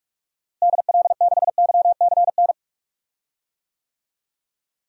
Morse code, heard as a keyed tone, about 41 wpm, 700 Hz. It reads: DL6YCN
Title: DXCC-100 Call Signs - 6 Characters - Repeated 3x - 45wpm